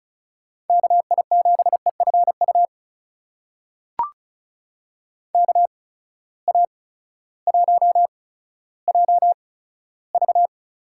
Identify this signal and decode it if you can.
Morse code, heard as a keyed tone, about 36 wpm, spaced as for 16 wpm, 700 Hz. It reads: KI7EFU KA1JV
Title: USA Call Signs - 6 Characters - 8x Character Spacing - 35wpm